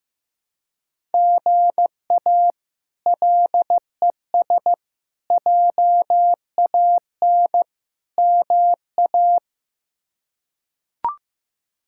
Morse code, heard as a keyed tone, about 15 wpm, 700 Hz. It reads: GA LES JAN MA
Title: K1USN SST QSO Practice - 15wpm